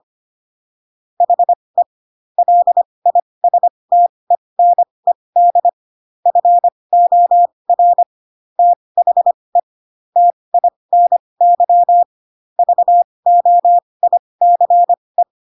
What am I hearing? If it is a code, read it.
Morse code, heard as a keyed tone, about 25 wpm, spaced as for 20 wpm, 700 Hz. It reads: HE LISTENED FOR THE TINY VOICE
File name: Pinocchio / Pinocchio ch1 15 25 ik1hge